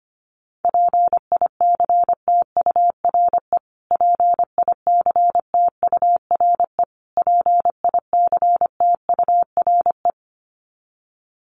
Morse code, heard as a keyed tone, about 25 wpm, 700 Hz. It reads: PICTURE PICTURE PICTURE
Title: Top 500 Words Repeated 25wpm